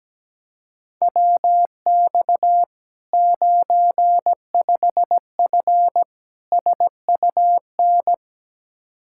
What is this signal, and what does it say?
Morse code, heard as a keyed tone, about 17 wpm, 700 Hz. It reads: WX 95F SUN
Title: Practice QSO - Exchange 2 - Info - 17wpm